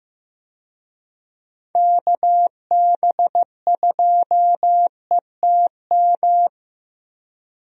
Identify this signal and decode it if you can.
Morse code, heard as a keyed tone, about 16 wpm, 700 Hz. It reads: KB2ETM
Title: USA Call Signs - 6 Characters - 15wpm